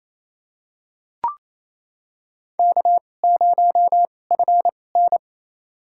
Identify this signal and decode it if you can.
Morse code, heard as a keyed tone, about 28 wpm, spaced as for 20 wpm, 700 Hz. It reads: K0FN
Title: USA Call Signs - 4 Characters - 2x Character Spacing - 28wpm